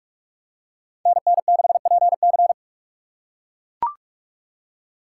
Morse code, heard as a keyed tone, about 43 wpm, 700 Hz. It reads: NN6PC
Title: USA Call Signs - 5 Characters - 4x Character Spacing - 45wpm